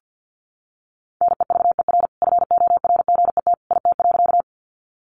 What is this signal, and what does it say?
Morse code, heard as a keyed tone, about 49 wpm, 700 Hz. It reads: NEVER FORGET IT.
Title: Sentences from Top 1200 Words 50wpm